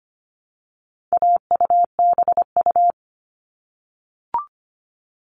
Morse code, heard as a keyed tone, about 25 wpm, 700 Hz. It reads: AUBU
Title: All Cut Numbers - 4-Digit 25wpm